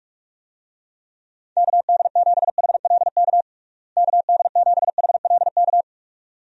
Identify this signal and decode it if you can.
Morse code, heard as a keyed tone, about 41 wpm, 700 Hz. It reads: KD7HLK KD7HLK
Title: USA Call Signs - 6 Characters - Repeated 2x - 45wpm